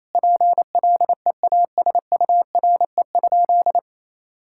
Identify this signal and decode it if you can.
Morse code, heard as a keyed tone, about 28 wpm, 700 Hz. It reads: PLEASURE?
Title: Sentences from Top 1600 Words 28wpm